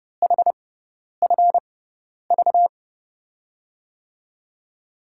Morse code, heard as a keyed tone, about 30 wpm, spaced as for 10 wpm, 700 Hz. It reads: HFV
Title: USA Call Signs - 5 Characters - 6x Character Spacing - 30wpm